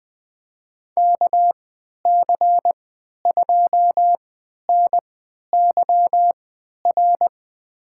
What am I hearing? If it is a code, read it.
Morse code, heard as a keyed tone, about 21 wpm, spaced as for 11 wpm, 700 Hz. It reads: KC2NYR
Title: USA Call Signs - 6 Characters - 3x Character Spacing - 20wpm